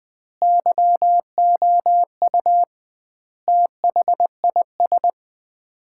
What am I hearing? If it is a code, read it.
Morse code, heard as a keyed tone, about 20 wpm, 700 Hz. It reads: YOU THIS
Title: 2x Word-Spacing - Sets of 5 Words - 20wpm